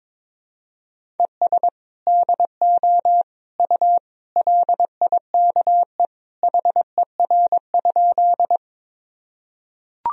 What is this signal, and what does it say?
Morse code, heard as a keyed tone, about 22 wpm, 700 Hz. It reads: ES DO U LIKE HER?
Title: Sentences from Top 100 Words and Abbreviations 22wpm